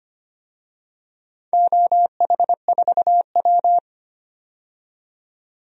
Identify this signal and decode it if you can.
Morse code, heard as a keyed tone, about 25 wpm, 700 Hz. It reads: OH4W
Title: DXCC-100 Call Signs - 4 Characters - 25wpm